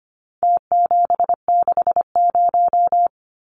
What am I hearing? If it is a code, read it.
Morse code, heard as a keyed tone, about 25 wpm, 700 Hz. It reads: T760
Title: Numbers - Contest Style - 4-Digit 25wpm